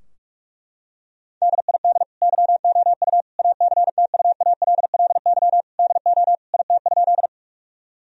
Morse code, heard as a keyed tone, about 41 wpm, 700 Hz. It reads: DID YOU ACTUALLY DO IT?
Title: Sentences from Top 1000 Words - Review Part 4 - 45wpm